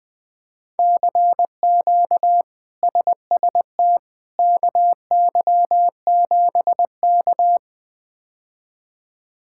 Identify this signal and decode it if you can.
Morse code, heard as a keyed tone, about 20 wpm, 700 Hz. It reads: CQ SST KY7K
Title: K1USN SST QSO Practice - 20wpm